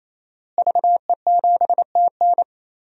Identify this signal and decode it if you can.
Morse code, heard as a keyed tone, about 28 wpm, 690 Hz. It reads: VE7TN
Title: DXCC-100 Call Signs - 5 Characters - Repeated 3x - 28wpm